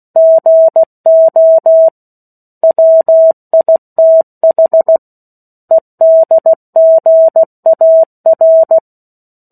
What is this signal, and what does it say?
Morse code, heard as a keyed tone, about 16 wpm, 650 Hz. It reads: GO WITH EDGAR
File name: 03-Chapter-03 16 WPM